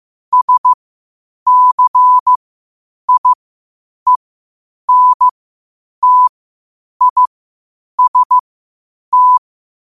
Morse code, heard as a keyed tone, about 15 wpm, spaced as for 8 wpm, 1000 Hz. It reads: SCIENTIST
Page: New quotes every day in morse code at 8 Words per minute.